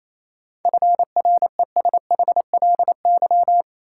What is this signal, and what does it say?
Morse code, heard as a keyed tone, about 28 wpm, 700 Hz. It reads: FRESHLY
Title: Sets of 5 Words Encore - New Word Drill - 28wpm